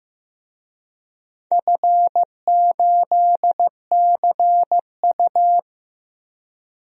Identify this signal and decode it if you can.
Morse code, heard as a keyed tone, about 15 wpm, 700 Hz. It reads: F8CU
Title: DXCC-100 Call Signs - 4 Characters - 15wpm